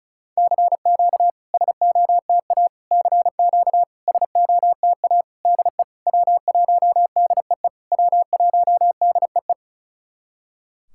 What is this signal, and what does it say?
Morse code, heard as a keyed tone, about 36 wpm, 690 Hz. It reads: CQ SOTA CQ SOTA DE W1DEE W1DEE
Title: SOTA QSO Practice 35wpm